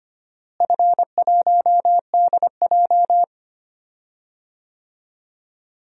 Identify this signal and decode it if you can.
Morse code, heard as a keyed tone, about 25 wpm, 700 Hz. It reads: F1DJ
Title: DXCC-100 Call Signs - 4 Characters - Repeated 3x - 25wpm